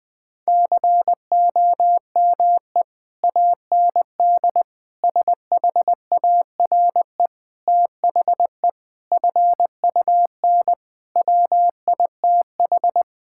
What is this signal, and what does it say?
Morse code, heard as a keyed tone, about 20 wpm, 700 Hz. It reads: COME AND SHARE THE FUN WITH
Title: Sentences from Top 1000 Words - Review Part 6 - 20wpm